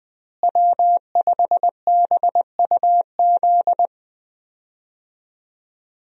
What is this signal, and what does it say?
Morse code, heard as a keyed tone, about 20 wpm, 700 Hz. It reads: W5BUZ
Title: USA Call Signs - 5 Characters - Repeated 2x - 20wpm